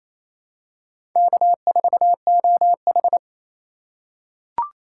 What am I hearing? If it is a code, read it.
Morse code, heard as a keyed tone, about 28 wpm, 700 Hz. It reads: K4OH
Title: USA Call Signs - 3x Character Spacing - 28wpm